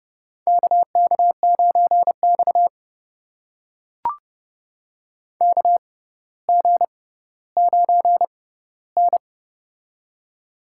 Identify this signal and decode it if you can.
Morse code, heard as a keyed tone, about 30 wpm, spaced as for 15 wpm, 700 Hz. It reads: KK9X KG9N
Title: USA Call Signs - 4 Characters - 6x Character Spacing - 30wpm